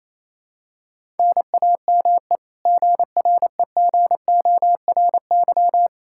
Morse code, heard as a keyed tone, about 28 wpm, 700 Hz. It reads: NAME GREGORY
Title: Practice QSO - Exchange 1 - Beginning - 28wpm